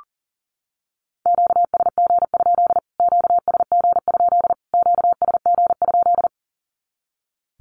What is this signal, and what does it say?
Morse code, heard as a keyed tone, about 40 wpm, 700 Hz. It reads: QSG? QSG? QSG?
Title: Q-Codes Repeated 3x 40wpm